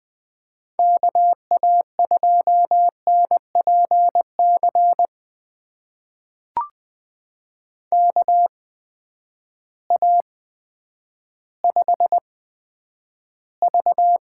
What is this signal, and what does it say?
Morse code, heard as a keyed tone, about 20 wpm, spaced as for 10 wpm, 700 Hz. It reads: KA2NPC KA5V
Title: USA Call Signs - 6 Characters - 8x Character Spacing - 20wpm